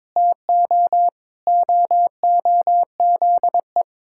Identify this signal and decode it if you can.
Morse code, heard as a keyed tone, about 22 wpm, 700 Hz. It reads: TO OOZE
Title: Pangram Challenge v2 - 22wpm